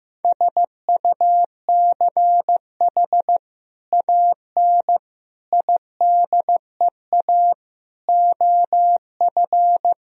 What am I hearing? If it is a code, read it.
Morse code, heard as a keyed tone, about 15 wpm, 700 Hz. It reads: SUCH AN IDEA OF
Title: Sentences from Top 1000 Words - Review Part 3 - 15wpm